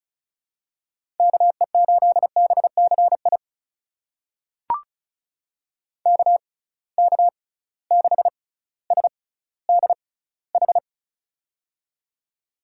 Morse code, heard as a keyed tone, about 36 wpm, spaced as for 18 wpm, 700 Hz. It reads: KE8BCI KK6SDH
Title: USA Call Signs - 6 Characters - 6x Character Spacing - 35wpm